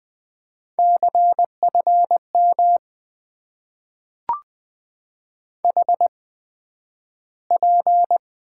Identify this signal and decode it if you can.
Morse code, heard as a keyed tone, about 20 wpm, spaced as for 11 wpm, 700 Hz. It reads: CFM HP
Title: Top 90 QSO Elements - 8x Character Spacing - 20wpm